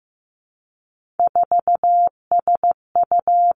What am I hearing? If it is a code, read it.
Morse code, heard as a keyed tone, about 15 wpm, 700 Hz. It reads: 4SU
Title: Three Challenging Letter-Number Combinations 15wpm